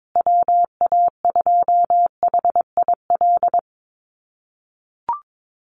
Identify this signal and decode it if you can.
Morse code, heard as a keyed tone, about 22 wpm, 700 Hz. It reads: WA2HIL
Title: USA Call Signs Repeated 3x 22wpm